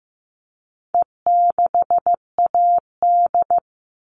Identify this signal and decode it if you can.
Morse code, heard as a keyed tone, about 15 wpm, 700 Hz. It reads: E6AD
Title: All Cut Numbers - 4-Digit 15wpm